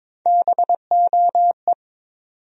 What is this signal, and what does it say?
Morse code, heard as a keyed tone, about 21 wpm, 700 Hz. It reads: BOE
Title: USA Call Sign Postfixes - 3 Letters 22wpm